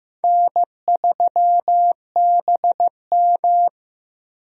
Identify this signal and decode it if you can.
Morse code, heard as a keyed tone, about 16 wpm, 700 Hz. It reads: N3BM
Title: USA Call Signs - 4 Characters - Repeated 2x - 15wpm